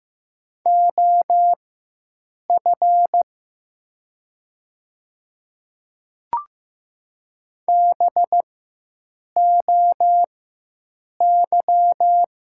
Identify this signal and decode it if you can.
Morse code, heard as a keyed tone, about 15 wpm, spaced as for 7 wpm, 700 Hz. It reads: OF BOY
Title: 4x Character Spacing - Top 200 Words - 15wpm